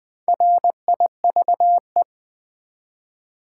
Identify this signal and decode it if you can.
Morse code, heard as a keyed tone, about 20 wpm, 700 Hz. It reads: RIVE
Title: Build A Word - 1 Letter At A Time - 5-Letter Words - 20wpm